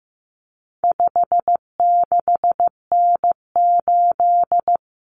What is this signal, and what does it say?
Morse code, heard as a keyed tone, about 15 wpm, 700 Hz. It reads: 56N8
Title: Common Cut Numbers - 4-Digit 15wpm